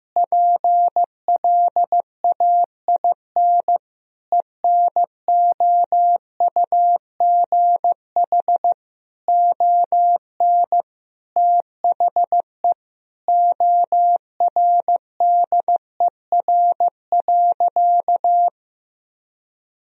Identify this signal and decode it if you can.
Morse code, heard as a keyed tone, about 16 wpm, 700 Hz. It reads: PLAIN ENOUGH ON THE ORDER.
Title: Sentences from Top 1000 Words - Review Part 2 - 15wpm